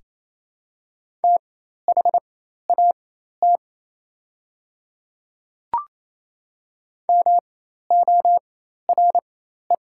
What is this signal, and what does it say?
Morse code, heard as a keyed tone, about 28 wpm, spaced as for 13 wpm, 690 Hz. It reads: THAT MORE
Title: 4x Character Spacing - Top 100 Words - 28wpm